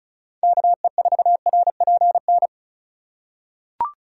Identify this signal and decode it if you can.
Morse code, heard as a keyed tone, about 36 wpm, 700 Hz. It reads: KE4RPN
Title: USA Call Signs - 6 Characters - 2x Character Spacing - 35wpm